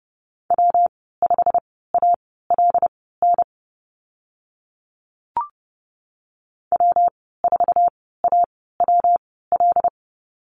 Morse code, heard as a keyed tone, about 30 wpm, spaced as for 17 wpm, 700 Hz. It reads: W5ALN W4AWL
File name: Lesson 19g - Letter w - Call Signs 30wpm